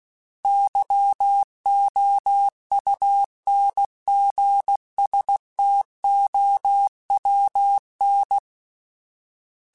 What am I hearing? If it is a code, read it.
Morse code, heard as a keyed tone, about 16 wpm, 800 Hz. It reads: YOUNGSTOWN
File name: fromSMs / FromSMs mp3s / Cities@15